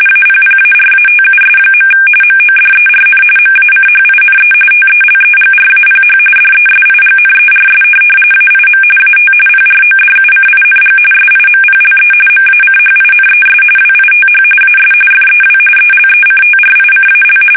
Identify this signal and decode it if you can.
RTTY, about 45.45 baud. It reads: APPARATUS. MODERN MACHINE TELEGRAPH DESIGNERS AIM AT PRODUCING SIMPLE MACHINES WITH AS FEW ELECTRICAL